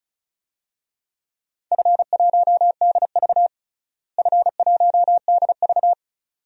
Morse code, heard as a keyed tone, about 36 wpm, 700 Hz. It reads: F1DV F1DV
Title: DXCC-100 Call Signs - 4 Characters - Repeated 2x - 35wpm